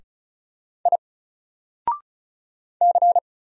Morse code, heard as a keyed tone, about 33 wpm, 700 Hz. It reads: I C
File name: Single-Letter-Number-35wpm